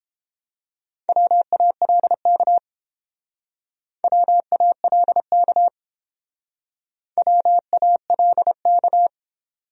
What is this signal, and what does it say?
Morse code, heard as a keyed tone, about 29 wpm, 700 Hz. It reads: WALK WALK WALK